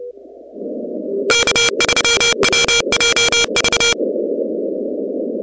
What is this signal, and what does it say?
Morse code, heard as a keyed tone, about 29 wpm, 2350 Hz, with noise. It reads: K3WJV
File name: contests / 2019-cw-wpx-kt5j-k5pi- -2330 0-17